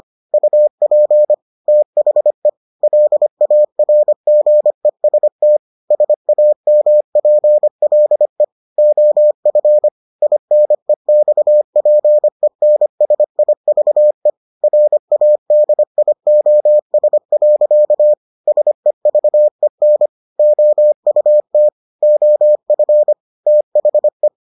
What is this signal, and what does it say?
Morse code, heard as a keyed tone, about 25 wpm, 600 Hz. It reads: UP THE LARGEST SAMPLE OF INEXPENSIVE RADIOS. SEVEN OUT OF THE